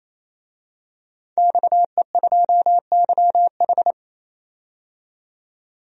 Morse code, heard as a keyed tone, about 28 wpm, 700 Hz. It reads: XE2YH
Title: DXCC-100 Call Signs - 5 Characters - 28wpm